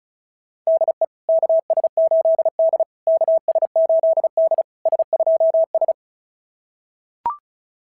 Morse code, heard as a keyed tone, about 36 wpm, 660 Hz. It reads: DE KS8D KS8D S2S
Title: SOTA QSO Practice 35wpm